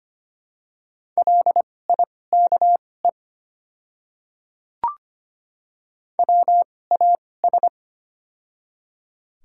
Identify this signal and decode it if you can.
Morse code, heard as a keyed tone, about 25 wpm, spaced as for 18 wpm, 700 Hz. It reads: LIKE WAS
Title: 2x Character Spacing - Top 100 Words - 25wpm